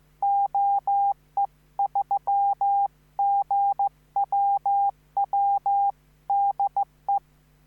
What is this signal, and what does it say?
Morse code, heard as a keyed tone, about 15 wpm, 800 Hz. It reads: OE3GWW DE
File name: QSO03